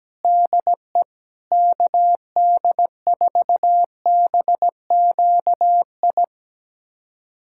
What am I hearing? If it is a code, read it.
Morse code, heard as a keyed tone, about 17 wpm, 700 Hz. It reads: DE KD4BQI
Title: USA Callsign DE Callsign - 6 Characters - 17wpm